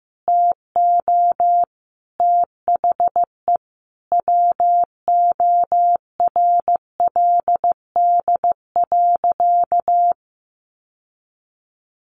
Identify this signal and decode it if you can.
Morse code, heard as a keyed tone, about 15 wpm, 700 Hz. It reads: TO THE WORLD.
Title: Sentences from Top 1200 Words 15wpm